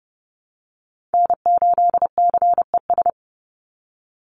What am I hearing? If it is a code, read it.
Morse code, heard as a keyed tone, about 29 wpm, 700 Hz. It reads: N8CES
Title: USA Call Signs 30wpm